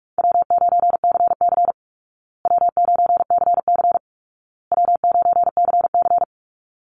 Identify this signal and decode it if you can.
Morse code, heard as a keyed tone, about 44 wpm, 700 Hz. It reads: W9CC W9CC W9CC
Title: Easy Call Signs Repeated 3x - 45wpm